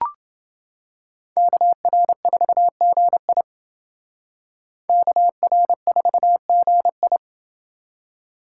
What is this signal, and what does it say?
Morse code, heard as a keyed tone, about 29 wpm, 700 Hz. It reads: KR4GI KR4GI